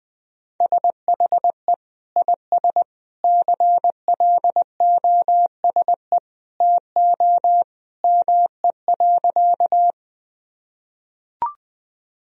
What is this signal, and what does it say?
Morse code, heard as a keyed tone, about 20 wpm, 700 Hz. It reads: SHE IS CLOSE TO ME.
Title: Sentences from Top 1000 Words - Review Part 1 - 20wpm